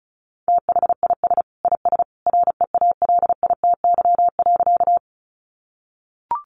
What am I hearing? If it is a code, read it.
Morse code, heard as a keyed tone, about 34 wpm, 700 Hz. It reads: THIS IS REALITY.
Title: Sentences from Top 1200 Words 35wpm